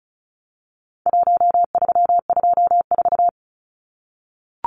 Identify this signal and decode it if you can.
Morse code, heard as a keyed tone, about 34 wpm, 700 Hz. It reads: 1324
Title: Numbers - Contest Style - 4-Digit 35wpm